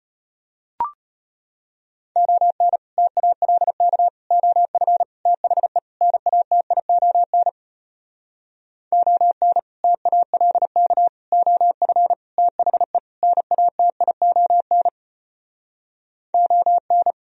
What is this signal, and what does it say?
Morse code, heard as a keyed tone, about 33 wpm, 700 Hz. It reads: ON TALK OF THE NATION ON TALK OF THE NATION ON